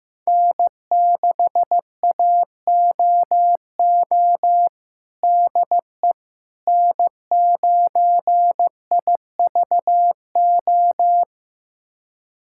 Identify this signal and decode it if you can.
Morse code, heard as a keyed tone, about 15 wpm, 700 Hz. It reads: N6AOO DE N9IVO
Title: USA Callsign DE Callsign - 5 Characters - 15wpm